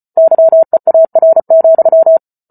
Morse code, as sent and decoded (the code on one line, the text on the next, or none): -.-- . .- .-. --..--
YEAR,